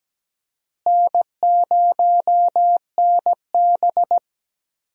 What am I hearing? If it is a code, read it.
Morse code, heard as a keyed tone, about 17 wpm, 700 Hz. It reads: N0NB
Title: USA Call Signs - 4 Characters - Repeated 2x - 17wpm